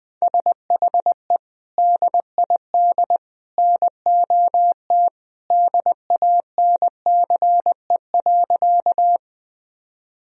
Morse code, heard as a keyed tone, about 20 wpm, 700 Hz. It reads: SHE DID NOT DANCE.
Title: Sentences from Top 1000 Words - Review Part 4 - 20wpm